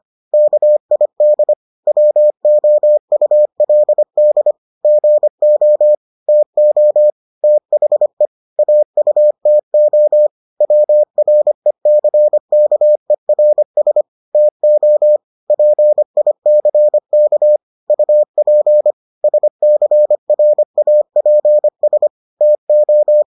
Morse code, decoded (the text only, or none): KID WOULD GO TO THE AUTO WRECKERS TO PICK UP SCRAPS TO